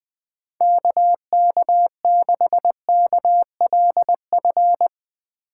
Morse code, heard as a keyed tone, about 20 wpm, 700 Hz. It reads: KK6KLF
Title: USA Call Signs - 8x Character Spacing - 20wpm